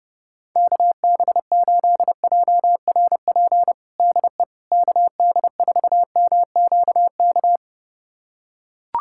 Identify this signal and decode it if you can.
Morse code, heard as a keyed tone, about 32 wpm, 700 Hz. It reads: KB8JRP DE KD4MQK
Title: USA Callsign DE Callsign - 6 Characters - 30wpm